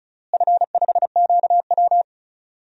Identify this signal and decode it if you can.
Morse code, heard as a keyed tone, about 36 wpm, 700 Hz. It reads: F5QW
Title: DXCC-100 Call Signs - 4 Characters - Repeated 3x - 35wpm